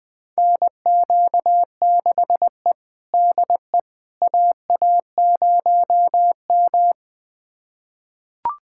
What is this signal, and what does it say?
Morse code, heard as a keyed tone, about 20 wpm, 700 Hz. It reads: NQ6E DE AA0M
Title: USA Callsign DE Callsign - 4 Characters - 20wpm